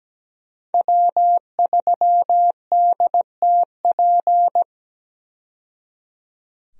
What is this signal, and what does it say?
Morse code, heard as a keyed tone, about 17 wpm, 700 Hz. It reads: W3DTP
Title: USA Call Signs - 5 Characters - 17wpm